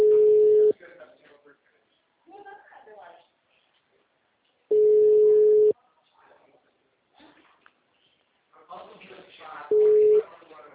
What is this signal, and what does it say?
Ringback tone (Europe, 425 Hz), with speech.